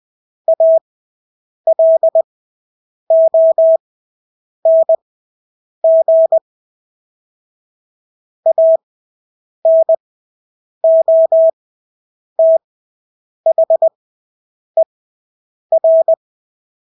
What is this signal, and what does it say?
Morse code, heard as a keyed tone, about 20 wpm, spaced as for 8 wpm, 650 Hz. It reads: ALONG ANOTHER
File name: Engelske ord 20/8 wpm